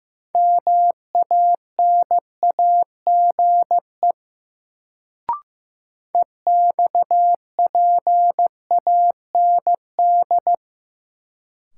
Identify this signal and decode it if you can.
Morse code, heard as a keyed tone, about 15 wpm, 700 Hz. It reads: MANAGE EXPAND
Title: Sentences from Top 1400 Words - New Word Drill - 15wpm